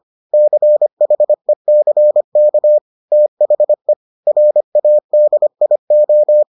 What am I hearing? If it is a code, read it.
Morse code, heard as a keyed tone, about 25 wpm, 600 Hz. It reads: CHECK THE RADIO